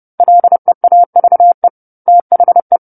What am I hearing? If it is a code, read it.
Morse code, heard as a keyed tone, about 30 wpm, 700 Hz. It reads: LEAVE THE
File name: 10-Chapter-10 30 WPM